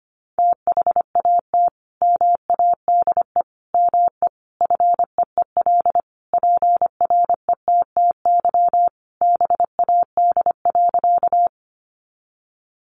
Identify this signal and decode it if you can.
Morse code, heard as a keyed tone, about 25 wpm, 700 Hz. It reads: THAT MADE ME FEEL PRETTY BAD.
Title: Sentences from Top 1000 Words 25wpm